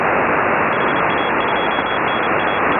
Morse code, heard as a keyed tone, about 32 wpm, 3350 Hz. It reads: HA2NAE